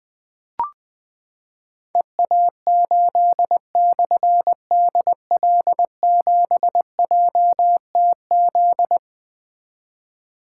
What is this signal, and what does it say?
Morse code, heard as a keyed tone, about 20 wpm, 700 Hz. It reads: EA8/DL7JTZ
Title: Difficult Call Signs 20wpm